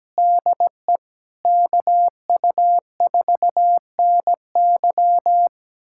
Morse code, heard as a keyed tone, about 17 wpm, 700 Hz. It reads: DE KU4NY
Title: USA Callsign DE Callsign - 5 Characters - 17wpm